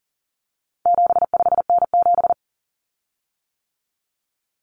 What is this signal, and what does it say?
Morse code, heard as a keyed tone, about 40 wpm, 700 Hz. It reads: 75N7
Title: Common Cut Numbers - 4-Digit 40wpm